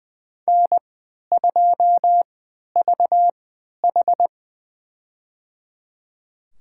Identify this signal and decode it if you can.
Morse code, heard as a keyed tone, about 20 wpm, spaced as for 11 wpm, 700 Hz. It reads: N2VH
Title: USA Call Signs - 4 Characters - 3x Character Spacing - 20wpm